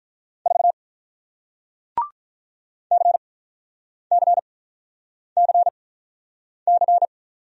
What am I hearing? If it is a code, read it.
Morse code, heard as a keyed tone, about 48 wpm, spaced as for 9 wpm, 700 Hz. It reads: 4 CCCC